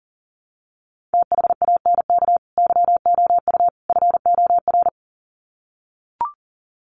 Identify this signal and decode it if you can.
Morse code, heard as a keyed tone, about 40 wpm, 700 Hz. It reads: THANK YOU FOR
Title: Sets of 3 Words 40wpm